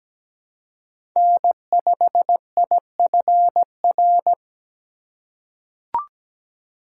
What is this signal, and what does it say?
Morse code, heard as a keyed tone, about 17 wpm, 700 Hz. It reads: N5IFR
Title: USA Call Signs - 5 Characters - 4x Character Spacing - 17wpm